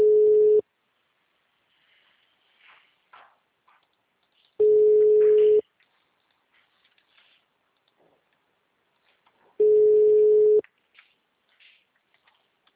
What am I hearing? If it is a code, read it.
Ringback tone (Europe, 425 Hz).